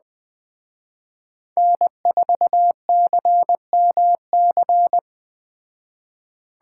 Morse code, heard as a keyed tone, about 20 wpm, 700 Hz. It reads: N4CMC